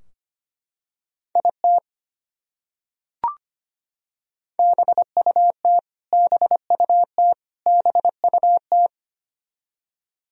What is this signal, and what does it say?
Morse code, heard as a keyed tone, about 25 wpm, 700 Hz. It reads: IT BUT BUT BUT
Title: Top 100 Words Repeated 25wpm